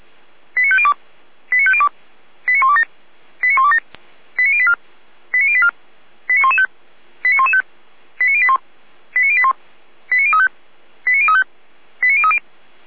Five-tone selective call (ZVEI), heard as quick runs of five tones, tones of about 70 ms. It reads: 89502 89502 89127 89127 89084 89084 89115 89115 89081 89081 89035 89035 89030
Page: SDR/ZVEI